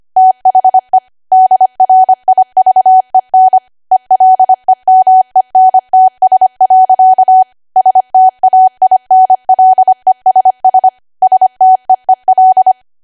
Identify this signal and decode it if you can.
Morse code, heard as a keyed tone, about 25 wpm, 750 Hz. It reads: THE DRIVEN ELEMENTS. STAINLESS STEEL
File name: Archive / 25 WPM / 25WPM